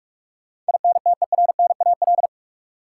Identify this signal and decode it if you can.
Morse code, heard as a keyed tone, about 46 wpm, 700 Hz. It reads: INTERNAL
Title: Sentences from Top 1400 Words - New Word Drill - 45wpm